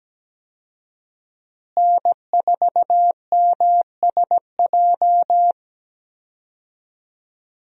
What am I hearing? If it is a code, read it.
Morse code, heard as a keyed tone, about 17 wpm, 700 Hz. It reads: N4MSJ
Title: USA Call Signs - 5 Characters - Repeated 2x - 17wpm